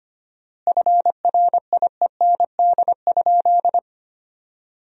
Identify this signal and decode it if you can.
Morse code, heard as a keyed tone, about 25 wpm, 700 Hz. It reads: FRIEND?
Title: 4x Word-Spacing - Sentences from Top 300 Words - 25wpm